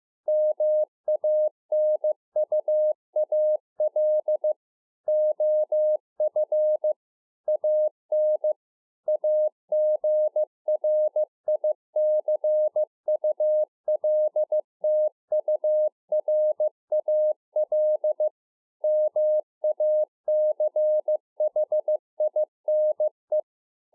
Morse code, heard as a keyed tone, about 15 wpm, 610 Hz. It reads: MANUAL OF AN AGRICULTURAL MACHINE